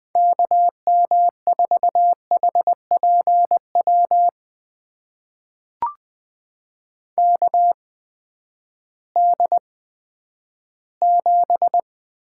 Morse code, heard as a keyed tone, about 20 wpm, spaced as for 11 wpm, 700 Hz. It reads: KM4HPW KD7
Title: USA Call Signs - 6 Characters - 8x Character Spacing - 20wpm